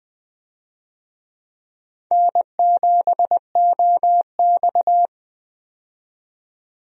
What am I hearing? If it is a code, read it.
Morse code, heard as a keyed tone, about 20 wpm, 700 Hz. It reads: N7OX
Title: USA Call Signs - 4 Characters - 20wpm